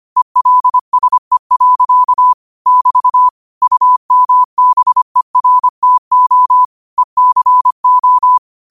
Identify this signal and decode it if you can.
Morse code, heard as a keyed tone, about 25 wpm, 1000 Hz. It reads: ELSE. <BT> UMBERTO ECO